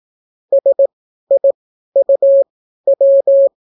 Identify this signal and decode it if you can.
Morse code, heard as a keyed tone, about 18 wpm, spaced as for 12 wpm, 550 Hz. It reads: SIUW